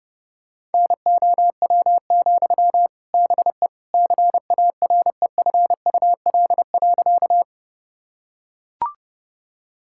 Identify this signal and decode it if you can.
Morse code, heard as a keyed tone, about 32 wpm, 700 Hz. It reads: NOW, BE CAREFUL.
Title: Sentences from Top 1600 Words 30wpm